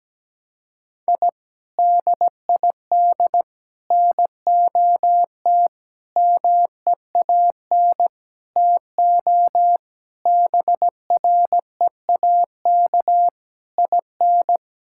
Morse code, heard as a keyed tone, about 17 wpm, 700 Hz. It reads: I DID NOT MEAN TO BREAK IN
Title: Sentences from Top 1000 Words - Review Part 6 - 17wpm